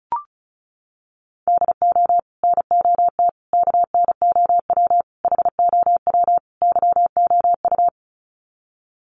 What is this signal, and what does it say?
Morse code, heard as a keyed tone, about 34 wpm, 700 Hz. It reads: DO NOT KNOW HOW YOU
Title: Sets of 5 Words Encore 35wpm